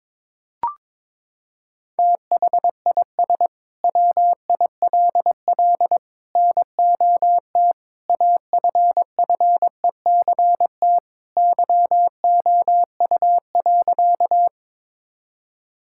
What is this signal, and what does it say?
Morse code, heard as a keyed tone, about 22 wpm, 700 Hz. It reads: THIS WILL NOT AFFECT YOU.
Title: Sentences from Top 1000 Words - Review Part 6 - 22wpm